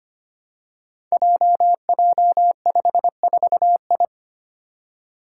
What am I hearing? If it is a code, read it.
Morse code, heard as a keyed tone, about 25 wpm, 700 Hz. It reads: JJ54I
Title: Five Challenging Letter-Number Combinations 25wpm